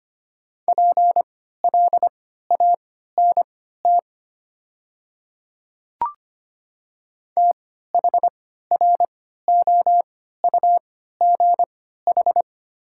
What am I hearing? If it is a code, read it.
Morse code, heard as a keyed tone, about 25 wpm, spaced as for 14 wpm, 700 Hz. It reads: PLANT THROUGH
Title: 3x Character Spacing - Top 300 Words - 25wpm